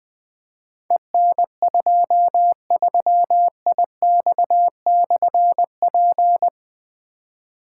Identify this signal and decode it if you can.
Morse code, heard as a keyed tone, about 20 wpm, 700 Hz. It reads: EN23IX/P
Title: Difficult Call Signs 20wpm